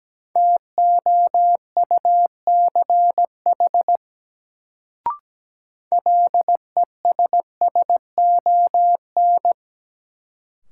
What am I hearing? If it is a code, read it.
Morse code, heard as a keyed tone, about 17 wpm, 700 Hz. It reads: TOUCH LESSON
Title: Sentences from Top 1200 Words - New Word Drill Alt - 17wpm